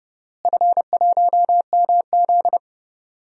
Morse code, heard as a keyed tone, about 30 wpm, 700 Hz. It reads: F1MZ
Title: DXCC-100 Call Signs - 4 Characters - Repeated 3x - 30wpm